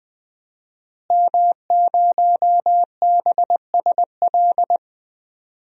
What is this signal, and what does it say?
Morse code, heard as a keyed tone, about 20 wpm, 700 Hz. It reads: M0BSL
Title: DXCC-100 Call Signs - 5 Characters - Repeated 2x - 20wpm